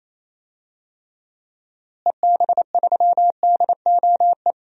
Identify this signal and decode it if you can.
Morse code, heard as a keyed tone, about 28 wpm, 700 Hz. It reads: EB3DOE
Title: DXCC-100 Call Signs - 6 Characters - Repeated 3x - 28wpm